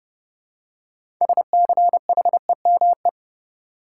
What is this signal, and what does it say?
Morse code, heard as a keyed tone, about 32 wpm, 700 Hz. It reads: SCHEME
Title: Six Letter Words - 30wpm